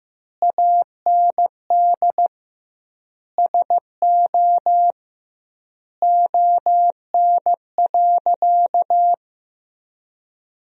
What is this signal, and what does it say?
Morse code, heard as a keyed tone, about 15 wpm, 700 Hz. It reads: AND SO ON.
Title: 2x Word-Spacing - Sentences from Top 100 Words - 15wpm